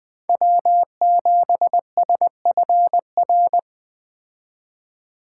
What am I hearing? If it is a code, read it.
Morse code, heard as a keyed tone, about 20 wpm, 700 Hz. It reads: W7SFR
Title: Speed Racing - USA 1x3 Call Signs 17wpm